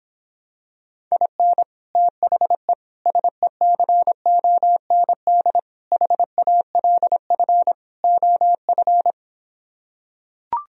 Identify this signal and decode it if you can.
Morse code, heard as a keyed tone, about 26 wpm, 700 Hz. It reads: IN THE SECOND HALF OF